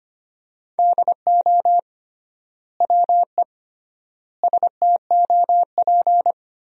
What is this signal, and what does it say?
Morse code, heard as a keyed tone, about 25 wpm, 700 Hz. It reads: DO WE STOP
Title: 3x Word-Spacing - Sentences from Top 300 Words - 25wpm